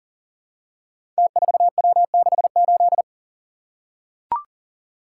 Morse code, heard as a keyed tone, about 37 wpm, 700 Hz. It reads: T4W68
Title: Code Groups - 5 Letter:Numbers - 40wpm